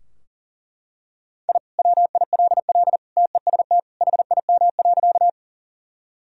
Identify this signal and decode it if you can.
Morse code, heard as a keyed tone, about 37 wpm, 700 Hz. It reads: I WILL TEST HIM.
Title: Sentences from Top 1000 Words - Review Part 3 - 40wpm